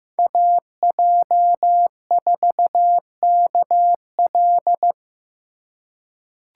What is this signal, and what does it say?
Morse code, heard as a keyed tone, about 16 wpm, 700 Hz. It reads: AJ4KL
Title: USA Call Signs - 5 Characters - 3x Character Spacing - 15wpm